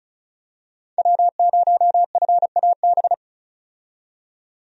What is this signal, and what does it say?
Morse code, heard as a keyed tone, about 34 wpm, 700 Hz. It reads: W0FAB
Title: USA Call Signs - 5 Characters - 35wpm